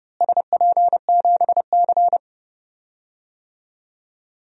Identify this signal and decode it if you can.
Morse code, heard as a keyed tone, about 30 wpm, 700 Hz. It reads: SP7C
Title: DXCC-100 Call Signs - 4 Characters - Repeated 3x - 30wpm